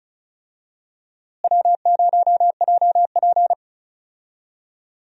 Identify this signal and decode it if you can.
Morse code, heard as a keyed tone, about 36 wpm, 700 Hz. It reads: W0JP
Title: USA Call Signs - 4 Characters - Repeated 2x - 35wpm